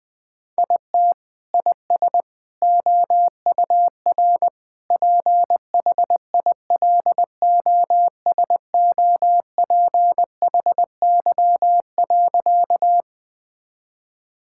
Morse code, heard as a keyed tone, about 20 wpm, 700 Hz. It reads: IT IS OUR PHILOSOPHY.
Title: Sentences from Top 1000 Words - Review Part 6 - 20wpm